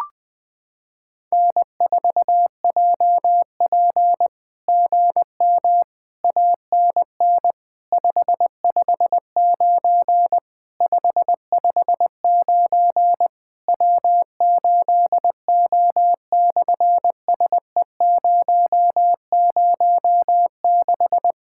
Morse code, heard as a keyed tone, about 20 wpm, 700 Hz. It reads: N4JP GM ANN 559 559 W8O/SE006
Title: SOTA QSO Practice 20wpm